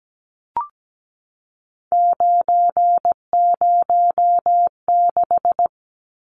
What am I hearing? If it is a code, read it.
Morse code, heard as a keyed tone, about 17 wpm, 700 Hz. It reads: 906
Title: Numbers - 3-Digit 17wpm